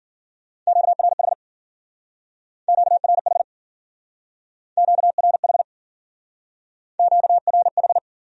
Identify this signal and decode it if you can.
Morse code, heard as a keyed tone, about 50 wpm, 700 Hz. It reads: QRH QRH QRH QRH
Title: Speed-Racing Q-Codes - 40wpm